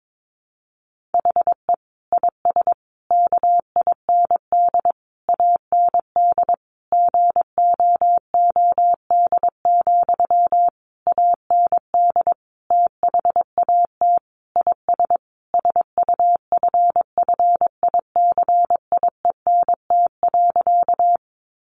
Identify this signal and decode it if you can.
Morse code, heard as a keyed tone, about 22 wpm, 700 Hz. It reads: HE IS KIND AND GOOD, AND THAT IS SUFFICIENT.
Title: Sentences from Top 1400 Words 22wpm